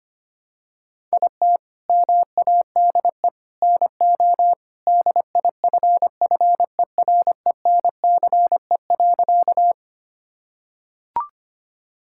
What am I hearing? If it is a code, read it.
Morse code, heard as a keyed tone, about 25 wpm, 700 Hz. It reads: IT MADE NO DIFFERENCE.
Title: Sentences from Top 1000 Words - Review Part 5 - 25wpm